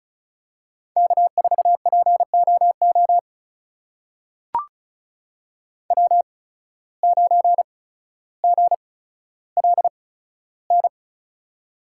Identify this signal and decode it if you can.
Morse code, heard as a keyed tone, about 36 wpm, spaced as for 15 wpm, 700 Hz. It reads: K4POO W9GLN
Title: USA Call Signs - 5 Characters - 8x Character Spacing - 35wpm